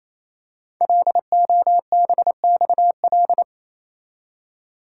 Morse code, heard as a keyed tone, about 28 wpm, 700 Hz. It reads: LOBXL
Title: Code Groups - 5 Letters - 28wpm